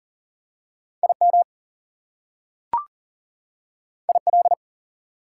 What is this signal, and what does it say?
Morse code, heard as a keyed tone, about 37 wpm, 700 Hz. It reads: IM IL
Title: Prefixes and Suffixes 40wpm